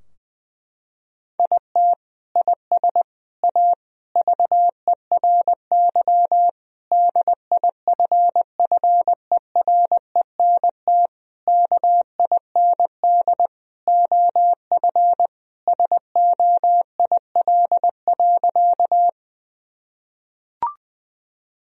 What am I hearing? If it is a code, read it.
Morse code, heard as a keyed tone, about 20 wpm, 700 Hz. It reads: IT IS A VERY DIFFERENT KIND OF SOIL.
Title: Sentences from Top 1000 Words - Review Part 4 - 20wpm